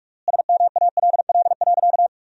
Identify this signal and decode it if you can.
Morse code, heard as a keyed tone, about 41 wpm, 700 Hz. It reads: SMALL.
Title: 3x Word-Spacing - Sentences from Top 400 Words - 45wpm